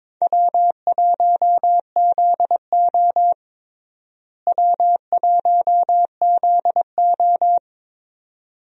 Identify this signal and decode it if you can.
Morse code, heard as a keyed tone, about 22 wpm, 700 Hz. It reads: W1ZO W1ZO
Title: USA Call Signs - 4 Characters - Repeated 2x - 22wpm